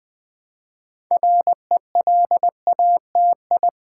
Morse code, heard as a keyed tone, about 20 wpm, 700 Hz. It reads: RELATI
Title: Build A Word - 3 Letters At A Time - 8-Letter Words - 20wpm